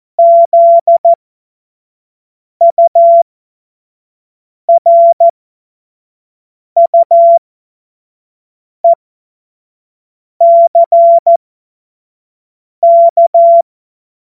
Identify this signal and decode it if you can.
Morse code, heard as a keyed tone, about 14 wpm, spaced as for 5 wpm, 680 Hz. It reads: ZURUECK
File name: Morsekurs / Witze / Witz 31. Januar 2023 25 F70